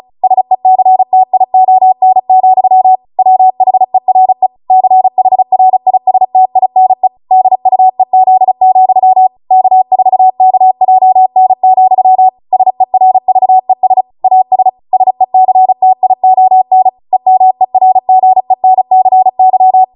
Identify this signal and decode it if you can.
Morse code, heard as a keyed tone, about 36 wpm, 750 Hz. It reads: SECTION, WHERE CHRISTINE DUEZ, K4KJN, SERVES AS SECTION EMERGENCY